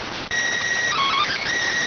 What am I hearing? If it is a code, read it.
SSTV (slow-scan television), PD 290.